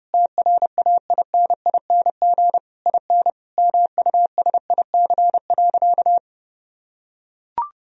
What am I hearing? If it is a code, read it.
Morse code, heard as a keyed tone, about 32 wpm, 700 Hz. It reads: TRAINING IN MUSIC.
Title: Sentences from Top 1000 Words - Review Part 4 - 30wpm